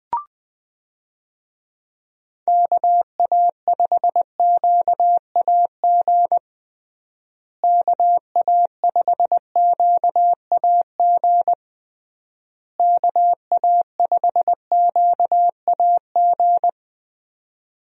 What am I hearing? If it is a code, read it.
Morse code, heard as a keyed tone, about 20 wpm, 700 Hz. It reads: KA5QAG KA5QAG KA5QAG
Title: USA Call Signs - 6 Characters - Repeated 3x - 20wpm